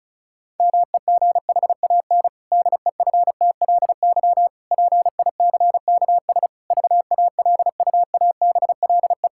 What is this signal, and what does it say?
Morse code, heard as a keyed tone, about 34 wpm, 700 Hz. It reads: MEGHAN DEFTLY PICKS VALUABLE
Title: Pangram Challenge v2 - 35wpm